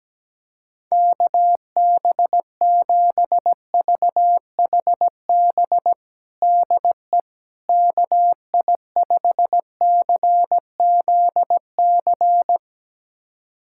Morse code, heard as a keyed tone, about 17 wpm, 700 Hz. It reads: KB7VHB DE KI5CZC
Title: USA Callsign DE Callsign - 6 Characters - 17wpm